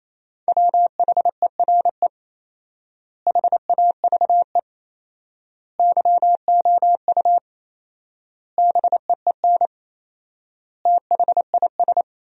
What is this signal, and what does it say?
Morse code, heard as a keyed tone, about 27 wpm, 700 Hz. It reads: WHERE HAVE YOU BEEN THIS
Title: 4x Word-Spacing - Sentences from Top 300 Words - 28wpm